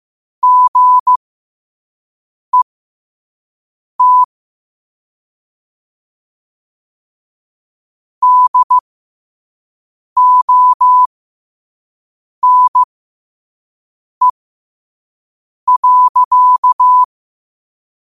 Morse code, heard as a keyed tone, about 15 wpm, spaced as for 5 wpm, 1000 Hz. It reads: GET DONE.
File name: qotdfiles / 5wpm